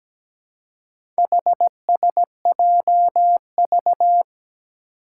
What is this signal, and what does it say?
Morse code, heard as a keyed tone, about 17 wpm, 700 Hz. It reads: HSJV
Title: Four Challenging Letter-Number Combinations 17wpm